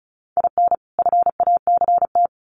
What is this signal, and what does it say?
Morse code, heard as a keyed tone, about 34 wpm, 700 Hz. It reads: IN FACT
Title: Sets of 2 Words 35wpm